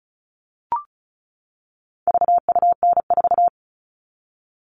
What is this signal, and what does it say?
Morse code, heard as a keyed tone, about 34 wpm, 700 Hz. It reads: VUN4
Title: All Cut Numbers - 4-Digit 35wpm